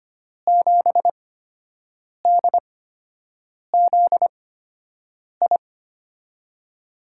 Morse code, heard as a keyed tone, about 25 wpm, spaced as for 7 wpm, 700 Hz. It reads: 7DZI
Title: USA Call Signs - 5 Characters - 8x Character Spacing - 25wpm